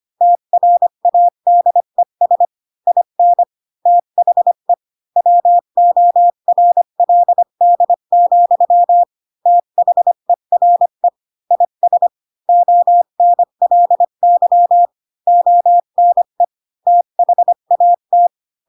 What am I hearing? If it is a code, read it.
Morse code, heard as a keyed tone, about 25 wpm, 700 Hz. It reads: TRADES IN THE WORLD, THERE IS ONLY ONE THAT
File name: Pinocchio / Pinocchio ch4 20 25 ik1hge